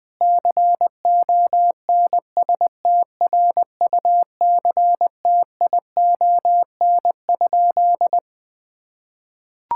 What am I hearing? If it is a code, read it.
Morse code, heard as a keyed tone, about 20 wpm, 700 Hz. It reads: CONSTRUCTION?
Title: Sentences from Top 1600 Words 20wpm